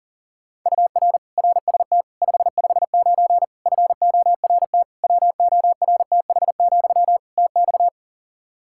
Morse code, heard as a keyed tone, about 40 wpm, 700 Hz. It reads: UR RST 559 FORT WORTH, TX
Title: Practice QSO - Exchange 1 - Beginning - 40wpm